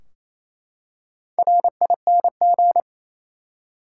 Morse code, heard as a keyed tone, about 28 wpm, 700 Hz. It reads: RING
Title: Four Letter Words - 28wpm